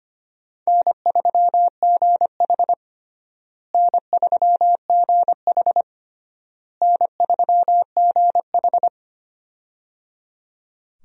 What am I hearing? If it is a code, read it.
Morse code, heard as a keyed tone, about 25 wpm, 700 Hz. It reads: N3GH N3GH N3GH
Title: USA Call Signs - 4 Characters - Repeated 3x - 25wpm